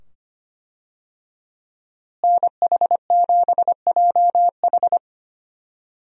Morse code, heard as a keyed tone, about 25 wpm, 700 Hz. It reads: NH7JH
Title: USA Call Signs - 5 Characters - Repeated 2x - 25wpm